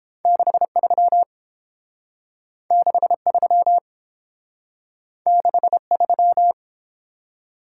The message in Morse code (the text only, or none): 63 63 63